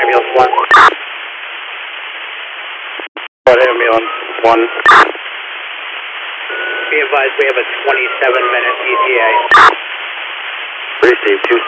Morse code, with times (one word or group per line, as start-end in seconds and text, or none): none